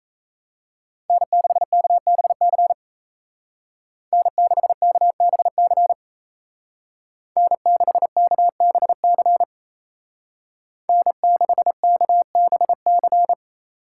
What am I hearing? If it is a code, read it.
Morse code, heard as a keyed tone, about 35 wpm, 700 Hz. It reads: N6KBC N6KBC N6KBC N6KBC
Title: Speed Racing - USA 1x3 Call Signs 28wpm